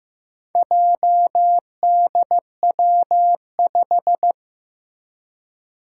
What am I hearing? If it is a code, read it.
Morse code, heard as a keyed tone, about 15 wpm, 700 Hz. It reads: JDW5
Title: Four Challenging Letter-Number Combinations 15wpm